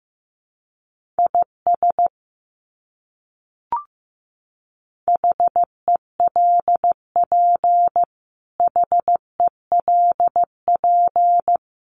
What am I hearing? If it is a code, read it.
Morse code, heard as a keyed tone, about 15 wpm, 700 Hz. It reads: IS HELP HELP
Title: Top 200 Words Repeated 15wpm